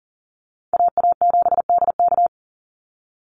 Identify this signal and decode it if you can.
Morse code, heard as a keyed tone, about 40 wpm, 700 Hz. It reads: AA7DK
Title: NA SOTA Chaser Callsigns 40wpm